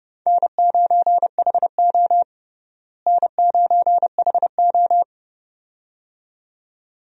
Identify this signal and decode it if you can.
Morse code, heard as a keyed tone, about 32 wpm, 700 Hz. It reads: N9HO N9HO
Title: USA Call Signs - 4 Characters - Repeated 2x - 30wpm